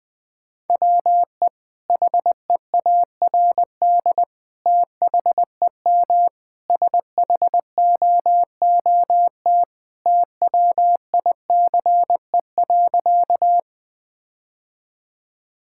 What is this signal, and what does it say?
Morse code, heard as a keyed tone, about 20 wpm, 700 Hz. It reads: WE HEARD THEM SHOOT TWICE.
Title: Sentences from Top 1600 Words 20wpm